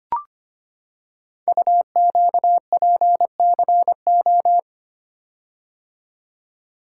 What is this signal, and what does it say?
Morse code, heard as a keyed tone, about 25 wpm, 700 Hz. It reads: UQPCO
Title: Code Groups - 5 Letters - 25wpm